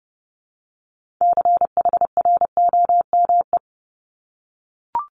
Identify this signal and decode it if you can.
Morse code, heard as a keyed tone, about 30 wpm, 700 Hz. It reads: CHROME
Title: Multi-letter Phonemes Repeated 3x 30wpm